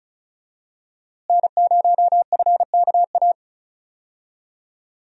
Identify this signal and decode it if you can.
Morse code, heard as a keyed tone, about 34 wpm, 700 Hz. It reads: N0FKA
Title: USA Call Signs - 5 Characters - Repeated 3x - 35wpm